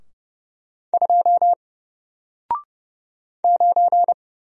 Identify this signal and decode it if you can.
Morse code, heard as a keyed tone, about 30 wpm, 700 Hz. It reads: 2 9
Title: Single Letter-Number Alt 30wpm